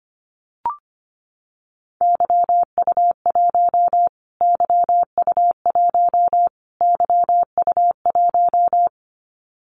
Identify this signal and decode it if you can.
Morse code, heard as a keyed tone, about 25 wpm, 700 Hz. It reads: YU1 YU1 YU1
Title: Callsign Prefixes for Top 25 DXCC - Repeated 3x - 25wpm